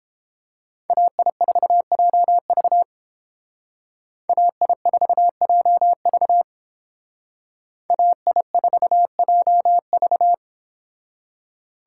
Morse code, heard as a keyed tone, about 30 wpm, 700 Hz. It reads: AI4JV AI4JV AI4JV